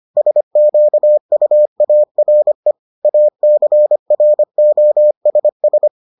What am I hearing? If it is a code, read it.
Morse code, heard as a keyed tone, about 25 wpm, 600 Hz. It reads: SQUARE ACROSS